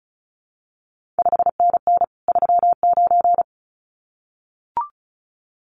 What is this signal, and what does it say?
Morse code, heard as a keyed tone, about 36 wpm, 700 Hz. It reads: 5NN 39
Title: CQ WW DX Contest Exchange - 35wpm